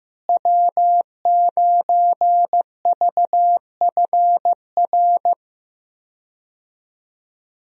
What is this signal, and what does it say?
Morse code, heard as a keyed tone, about 15 wpm, 700 Hz. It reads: W9VFR
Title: USA Call Signs - 5 Characters - Repeated 3x - 15wpm